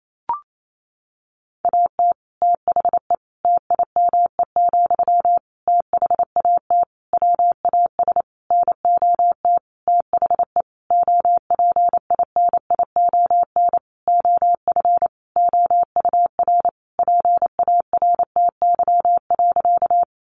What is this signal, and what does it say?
Morse code, heard as a keyed tone, about 28 wpm, 700 Hz. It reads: AT THE TIME, THAT WAS NOT THE OPINION OF OUR PARTY.
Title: Sentences from Top 1400 Words 28wpm